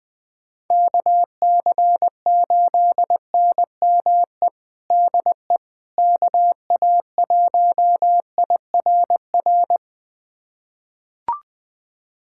Morse code, heard as a keyed tone, about 20 wpm, 700 Hz. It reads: KC8NME DE KA1IRR
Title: USA Callsign DE Callsign - 6 Characters - 20wpm